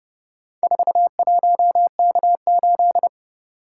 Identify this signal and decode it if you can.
Morse code, heard as a keyed tone, about 30 wpm, 700 Hz. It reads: 41K8